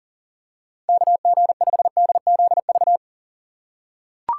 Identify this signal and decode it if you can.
Morse code, heard as a keyed tone, about 40 wpm, 700 Hz. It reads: KG5DZV
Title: USA Call Signs - 6 Characters - 3x Character Spacing - 40wpm